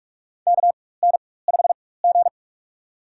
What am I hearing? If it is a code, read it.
Morse code, heard as a keyed tone, about 46 wpm, spaced as for 21 wpm, 700 Hz. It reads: KN5G
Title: USA Call Signs - 4 Characters - 4x Character Spacing - 45wpm